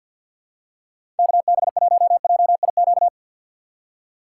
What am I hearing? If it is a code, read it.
Morse code, heard as a keyed tone, about 53 wpm, 700 Hz. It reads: KB1JIQ
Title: USA Call Signs - 6 Characters - 3x Character Spacing - 50wpm